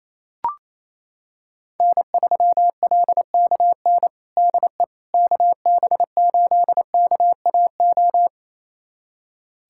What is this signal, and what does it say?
Morse code, heard as a keyed tone, about 28 wpm, 700 Hz. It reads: N3LKN DE KB8KAO
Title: USA Callsign DE Callsign 28wpm